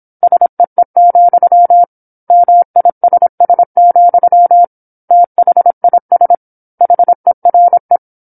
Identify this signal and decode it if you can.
Morse code, heard as a keyed tone, about 26 wpm, 700 Hz. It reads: SEE, MISS, THIS HERE
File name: 08-Chapter-08 26 WPM